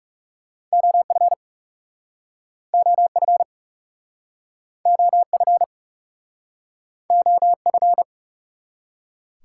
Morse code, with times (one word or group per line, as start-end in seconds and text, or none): 0.7-1.3 OF
2.7-3.4 OF
4.9-5.6 OF
7.1-8.0 OF